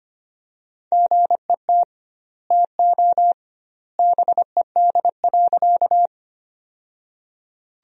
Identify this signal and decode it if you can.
Morse code, heard as a keyed tone, about 25 wpm, 700 Hz. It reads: GET TO BED.
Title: 2x Word-Spacing - Sentences from Top 500 Words - 25wpm